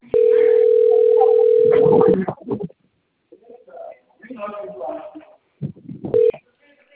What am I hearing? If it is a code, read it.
Ringback tone (North America), 440 + 480 Hz, with speech.
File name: out- - - - - .53457